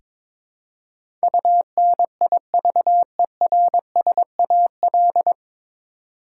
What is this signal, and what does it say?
Morse code, heard as a keyed tone, about 21 wpm, 700 Hz. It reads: UNIVERSAL
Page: Morse Code Ninja practice 22wpm